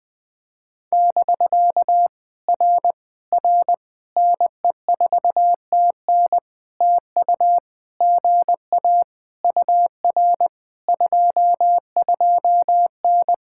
Morse code, heard as a keyed tone, about 19 wpm, 700 Hz. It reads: <BK> R R NE4TN TU GA UR 22N
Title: POTA QSO Practice - 20wpm